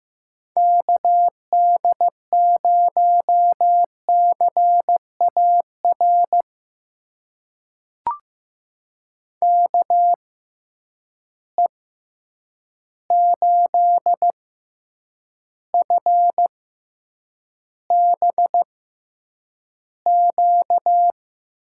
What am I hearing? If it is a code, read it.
Morse code, heard as a keyed tone, about 15 wpm, spaced as for 8 wpm, 700 Hz. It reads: KD0CAR KE8FBQ
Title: USA Call Signs - 6 Characters - 6x Character Spacing - 15wpm